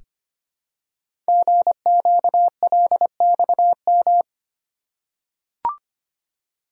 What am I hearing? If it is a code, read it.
Morse code, heard as a keyed tone, about 25 wpm, 700 Hz. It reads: GQLXM
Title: Code Groups - 5 Letters - 25wpm